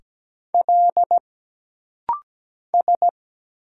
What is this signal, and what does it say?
Morse code, heard as a keyed tone, about 17 wpm, 700 Hz. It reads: L S
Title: Single Letter-Number - Rapid-Fire 17wpm